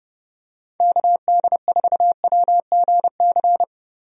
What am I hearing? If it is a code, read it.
Morse code, heard as a keyed tone, about 30 wpm, 700 Hz. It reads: KD4WGC
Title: USA Call Signs - 6 Characters - Repeated 3x - 30wpm